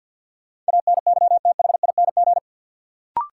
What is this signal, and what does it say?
Morse code, heard as a keyed tone, about 47 wpm, 700 Hz. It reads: ANYTHING
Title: Sets of 4 Words - New Word Drill - 50wpm